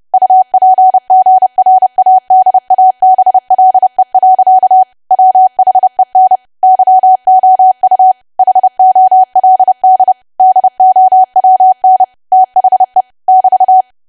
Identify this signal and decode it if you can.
Morse code, heard as a keyed tone, about 30 wpm, 750 Hz. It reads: UPGRADABLE. WHEN YOU HOLD DOWN THE <BT>
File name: cwa / W1AW-Level3 / 30WPM